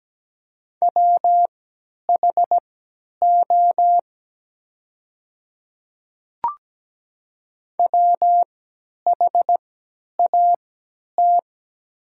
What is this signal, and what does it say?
Morse code, heard as a keyed tone, about 17 wpm, spaced as for 10 wpm, 700 Hz. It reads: WHO WHAT
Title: 3x Character Spacing - Top 100 Words - 17wpm